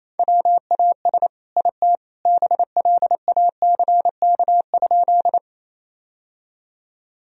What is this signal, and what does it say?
Morse code, heard as a keyed tone, about 28 wpm, 700 Hz. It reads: WAS IT BLACK?
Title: Sentences from Top 1000 Words - Review Part 2 - 28wpm